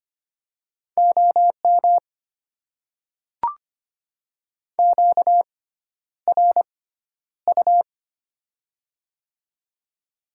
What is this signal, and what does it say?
Morse code, heard as a keyed tone, about 25 wpm, spaced as for 11 wpm, 690 Hz. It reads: OM QRU
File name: Top 90 QSO Elements - 6x Character Spacing - 25wpm